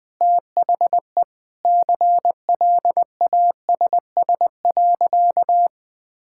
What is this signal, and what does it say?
Morse code, heard as a keyed tone, about 21 wpm, 700 Hz. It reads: THE CLASS.
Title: Sentences with 3-Word Phrases - Spoken Beginning - 20wpm